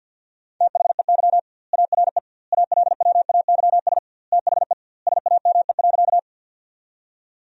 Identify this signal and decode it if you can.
Morse code, heard as a keyed tone, about 50 wpm, 700 Hz. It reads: THEY ARE ALWAYS THE SAME.
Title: 2x Word-Spacing - Sentences from Top 300 Words - 50wpm